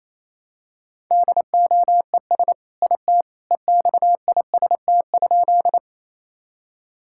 Morse code, heard as a keyed tone, about 28 wpm, 700 Hz. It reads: DOES IT EXIST?
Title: Sentences from Top 1000 Words - Review Part 5 - 28wpm